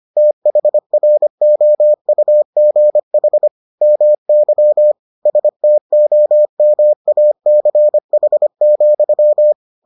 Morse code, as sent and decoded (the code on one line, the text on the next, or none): - .... .-. --- ..- --. .... / -- -.-- / ... - --- -- .- -.-. .... --..--
THROUGH MY STOMACH,